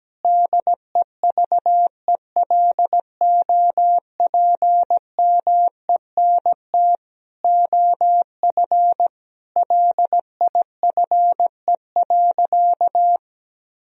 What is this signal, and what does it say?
Morse code, heard as a keyed tone, about 17 wpm, 700 Hz. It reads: DEVELOPMENT OF LIFE.
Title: Sentences from Top 1000 Words - Review Part 4 - 17wpm